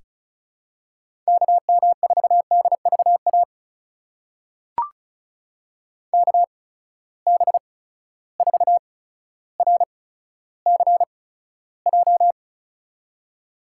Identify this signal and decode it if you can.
Morse code, heard as a keyed tone, about 36 wpm, spaced as for 15 wpm, 700 Hz. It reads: KM4DVA KB4RCJ
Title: USA Call Signs - 6 Characters - 8x Character Spacing - 35wpm